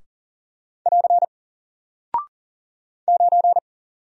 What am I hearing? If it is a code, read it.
Morse code, heard as a keyed tone, about 40 wpm, 700 Hz. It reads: <AR> 9
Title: Single Character (Letter:Number:Punctuation:Prosign) Alt 40wpm